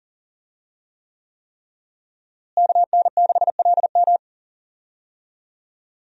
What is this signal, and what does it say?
Morse code, heard as a keyed tone, about 37 wpm, 700 Hz. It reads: KN6LM
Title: USA Call Signs - 5 Characters - Repeated 3x - 40wpm